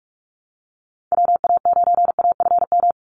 Morse code, heard as a keyed tone, about 46 wpm, 700 Hz. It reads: WA9AFM
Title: USA Call Signs 45wpm